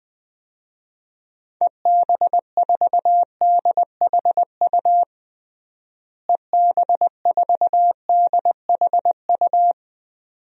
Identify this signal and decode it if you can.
Morse code, heard as a keyed tone, about 20 wpm, 700 Hz. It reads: EB4DHU EB4DHU
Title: DXCC-100 Call Signs - 6 Characters - Repeated 2x - 20wpm